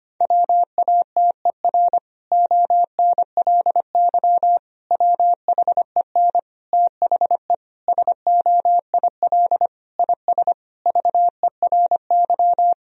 Morse code, heard as a keyed tone, about 25 wpm, 700 Hz. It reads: WATER ONLY WHEN THE SOIL IS VERY
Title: Sentences from Top 1000 Words - Review Part 4 - 25wpm